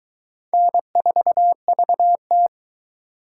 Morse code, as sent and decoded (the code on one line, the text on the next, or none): -. ....- ...- -
N4VT